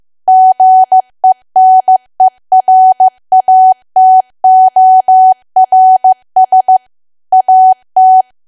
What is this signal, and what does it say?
Morse code, heard as a keyed tone, about 15 wpm, 750 Hz. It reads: GENERATORS AT
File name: cwa / W1AW-Level2 / 15WPM